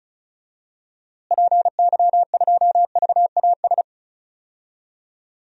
Morse code, heard as a keyed tone, about 36 wpm, 700 Hz. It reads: PY2VAS
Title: DXCC-100 Call Signs - 6 Characters - Repeated 2x - 35wpm